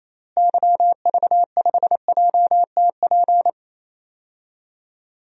Morse code, heard as a keyed tone, about 28 wpm, 700 Hz. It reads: YV5JTP
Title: DXCC-100 Call Signs - 6 Characters - Repeated 3x - 28wpm